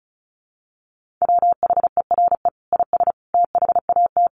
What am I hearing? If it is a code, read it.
Morse code, heard as a keyed tone, about 34 wpm, 700 Hz. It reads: WHERE IS THAT
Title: Sentences from Top 1500 Words 35wpm